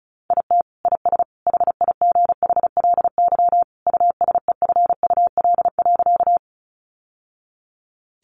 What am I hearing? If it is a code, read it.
Morse code, heard as a keyed tone, about 34 wpm, 700 Hz. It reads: IT IS HIGHLY USEFUL.
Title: Sentences from Top 1100 Words 35wpm